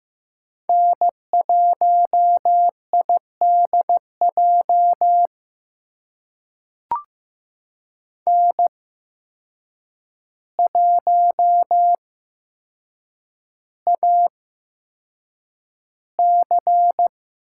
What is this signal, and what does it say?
Morse code, heard as a keyed tone, about 15 wpm, spaced as for 7 wpm, 700 Hz. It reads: N1IDJ N1AC
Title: USA Call Signs - 8x Character Spacing - 15wpm